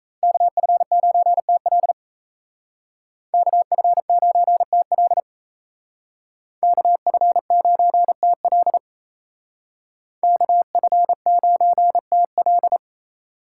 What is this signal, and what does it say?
Morse code, heard as a keyed tone, about 37 wpm, 700 Hz. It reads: KF9TL KF9TL KF9TL KF9TL